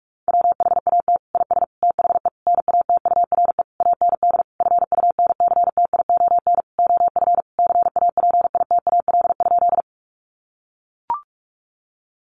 Morse code, heard as a keyed tone, about 44 wpm, 700 Hz. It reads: WHAT IS THE NATURE AND FUNCTION OF CAPITAL?
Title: Sentences from Top 1100 Words 45wpm